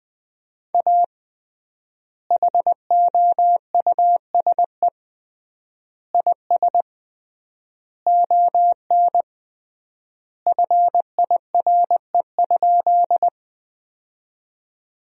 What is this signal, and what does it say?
Morse code, heard as a keyed tone, about 20 wpm, 700 Hz. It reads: A HOUSE IS ON FIRE?
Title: 3x Word-Spacing - Sentences from Top 400 Words - 20wpm